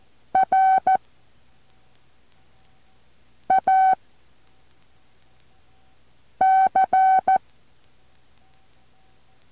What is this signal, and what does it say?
Morse code, heard as a keyed tone, about 14 wpm, 760 Hz. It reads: R A C
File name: intl morse code2